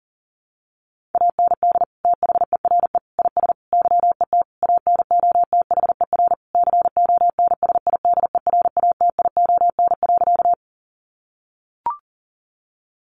Morse code, heard as a keyed tone, about 40 wpm, 700 Hz. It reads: AND THERE IS YET ANOTHER CONSIDERATION.
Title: Sentences from Top 1500 Words 40wpm